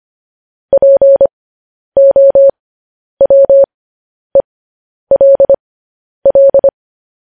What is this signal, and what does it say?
Morse code, heard as a keyed tone, about 25 wpm, spaced as for 10 wpm, 550 Hz. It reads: POWELL
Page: News Headlines in Morse Code at 10 WPM